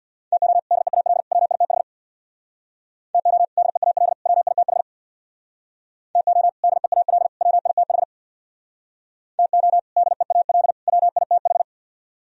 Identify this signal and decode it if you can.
Morse code, heard as a keyed tone, about 64 wpm, 700 Hz. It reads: TO DEAL WITH TO DEAL WITH TO DEAL WITH TO DEAL WITH